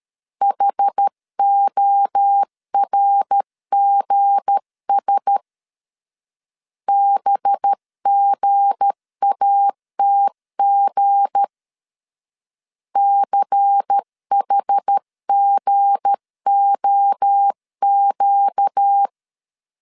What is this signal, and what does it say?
Morse code, heard as a keyed tone, about 13 wpm, 800 Hz. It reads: HORGS BGATG CHGOQ
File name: LEZIONI / 7 GF9Z / 7 G 12wpm 2055